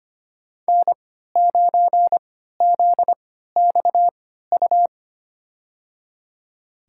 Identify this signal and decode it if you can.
Morse code, heard as a keyed tone, about 25 wpm, spaced as for 14 wpm, 700 Hz. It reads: N9ZXU
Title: USA Call Signs - 3x Character Spacing - 25wpm